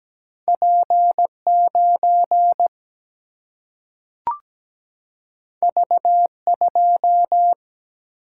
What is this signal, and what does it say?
Morse code, heard as a keyed tone, about 17 wpm, 700 Hz. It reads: P9 V2
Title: Single-Letter Single-Number 17wpm